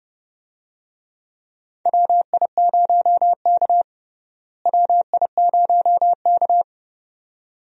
Morse code, heard as a keyed tone, about 32 wpm, 700 Hz. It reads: WI0K WI0K
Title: USA Call Signs - 4 Characters - Repeated 2x - 30wpm